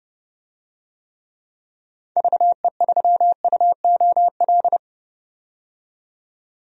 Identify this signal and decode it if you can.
Morse code, heard as a keyed tone, about 32 wpm, 700 Hz. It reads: VE3UOL
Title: DXCC-100 Call Signs - 6 Characters - 30wpm